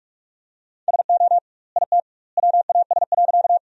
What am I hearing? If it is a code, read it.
Morse code, heard as a keyed tone, about 44 wpm, 700 Hz. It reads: SO IT WAS.
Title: 2x Word-Spacing - Sentences from Top 100 Words - 45wpm